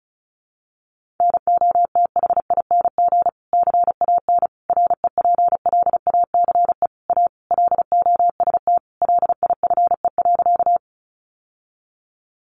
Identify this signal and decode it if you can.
Morse code, heard as a keyed tone, about 34 wpm, 700 Hz. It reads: NOTHING CAN REPLACE A LOST LIFE.
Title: Sentences from Top 1400 Words 35wpm